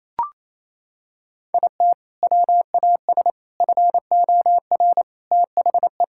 Morse code, heard as a keyed tone, about 28 wpm, 700 Hz. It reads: IT WAS FOR THE
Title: Sentences from Top 1000 Words - Review Part 6 - 28wpm